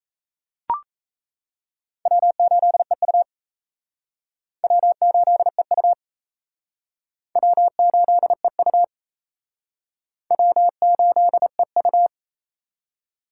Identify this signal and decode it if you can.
Morse code, heard as a keyed tone, about 37 wpm, 700 Hz. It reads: W8EU W8EU W8EU W8EU